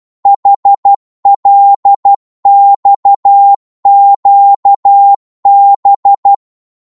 Morse code, heard as a keyed tone, about 12 wpm, 800 Hz. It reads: HLXQB
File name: Morse / ltr-12wpm